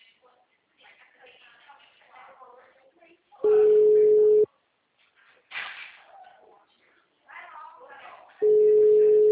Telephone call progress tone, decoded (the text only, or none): ringback tone